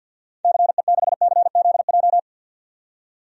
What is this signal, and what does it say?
Morse code, heard as a keyed tone, about 47 wpm, 700 Hz. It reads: CE6CZJ
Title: DXCC-100 Call Signs - 6 Characters - 50wpm